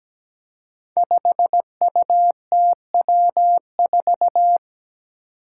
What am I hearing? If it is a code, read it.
Morse code, heard as a keyed tone, about 17 wpm, 700 Hz. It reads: 5UTW4
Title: Five Challenging Letter-Number Combinations 17wpm